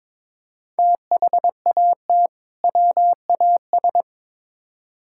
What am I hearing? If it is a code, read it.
Morse code, heard as a keyed tone, about 21 wpm, 700 Hz. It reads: THAT WAS
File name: practice / 22wpm / Build A Phrase - 5-Word Phrases - 22wpm